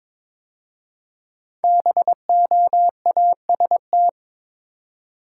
Morse code, heard as a keyed tone, about 22 wpm, 700 Hz. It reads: BOAST
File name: 5 Letter Words - First 3 Letters Given - 22wpm